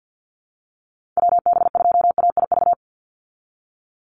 Morse code, heard as a keyed tone, about 49 wpm, 700 Hz. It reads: WB2AIV
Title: Call Signs Repeated 2x - 50wpm